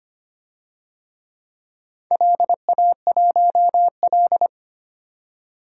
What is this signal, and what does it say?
Morse code, heard as a keyed tone, about 25 wpm, 700 Hz. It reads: LA1L
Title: DXCC-100 Call Signs - 4 Characters - 25wpm